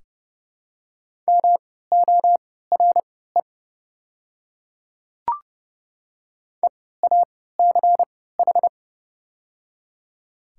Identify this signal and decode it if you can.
Morse code, heard as a keyed tone, about 30 wpm, spaced as for 17 wpm, 700 Hz. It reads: MORE EACH
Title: 3x Character Spacing - Top 100 Words - 30wpm